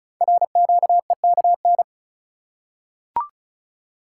Morse code, heard as a keyed tone, about 36 wpm, 700 Hz. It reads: RQEKN
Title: Code Groups - 5 Letters - 35wpm